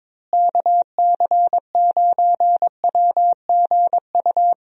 Morse code, heard as a keyed tone, about 22 wpm, 700 Hz. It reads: KC9WGU
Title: USA Call Signs - 6 Characters - Repeated 3x - 22wpm